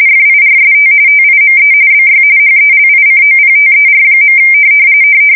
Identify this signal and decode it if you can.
RTTY, about 45.45 baud. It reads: FIRST TIME BY HELICOPTER.
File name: fsk170 / USS-New-Jersey